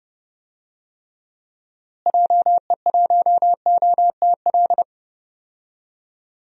Morse code, heard as a keyed tone, about 30 wpm, 700 Hz. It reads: JE1OTL
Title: DXCC-100 Call Signs - 6 Characters - Repeated 2x - 30wpm